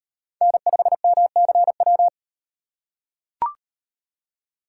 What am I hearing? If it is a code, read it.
Morse code, heard as a keyed tone, about 34 wpm, 700 Hz. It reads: N5MCW